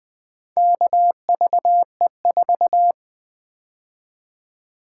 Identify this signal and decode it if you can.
Morse code, heard as a keyed tone, about 20 wpm, 700 Hz. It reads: KVE4
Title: Four Letter-Number Combinations - 20wpm